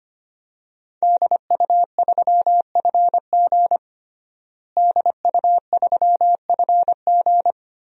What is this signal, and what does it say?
Morse code, heard as a keyed tone, about 25 wpm, 700 Hz. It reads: DU3FG DU3FG
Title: DXCC-100 Call Signs - 5 Characters - Repeated 2x - 25wpm